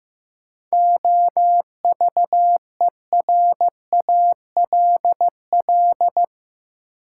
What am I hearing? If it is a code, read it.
Morse code, heard as a keyed tone, about 15 wpm, 700 Hz. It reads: OVERALL
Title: Sentences from Top 1000 Words - Review Part 7 - New Word Drill - 15wpm